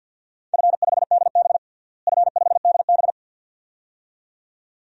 Morse code, heard as a keyed tone, about 46 wpm, 700 Hz. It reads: F5DB F5DB
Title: DXCC-100 Call Signs - 4 Characters - Repeated 2x - 50wpm